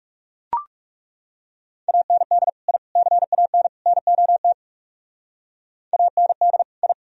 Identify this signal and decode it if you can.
Morse code, heard as a keyed tone, about 35 wpm, 700 Hz. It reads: AND I CAN NOT AND I
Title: Speed-Racing - Sets of 4 Words 30wpm